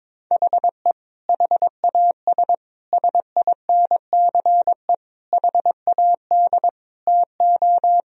Morse code, heard as a keyed tone, about 22 wpm, 700 Hz. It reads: HE HAS SINCE HAD TO
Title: Sentences from Top 1000 Words - Review Part 2 - 22wpm